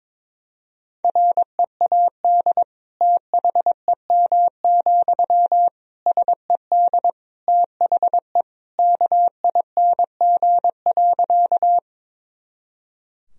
Morse code, heard as a keyed tone, about 22 wpm, 700 Hz. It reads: READ THEM, SED THE KING.
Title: Sentences from Top 400 Words and Abbreviations 22wpm